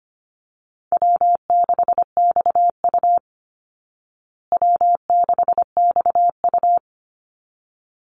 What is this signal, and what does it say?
Morse code, heard as a keyed tone, about 25 wpm, 700 Hz. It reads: W6XU W6XU
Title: Easy Call Signs Repeated 2x - 25wpm